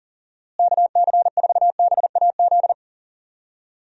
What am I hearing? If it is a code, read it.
Morse code, heard as a keyed tone, about 40 wpm, 700 Hz. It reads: KC4BAZ
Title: USA Call Signs - 6 Characters - Repeated 3x - 40wpm